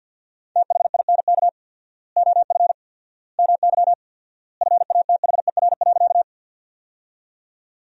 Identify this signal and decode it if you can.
Morse code, heard as a keyed tone, about 53 wpm, 700 Hz. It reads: THINK OF MY FATHER.
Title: 4x Word-Spacing - Sentences from Top 300 Words - 50wpm